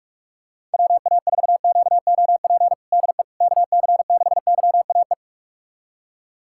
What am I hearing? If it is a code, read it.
Morse code, heard as a keyed tone, about 47 wpm, 700 Hz. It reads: WA4QOP DE KC6YAE
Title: USA Callsign DE Callsign 45wpm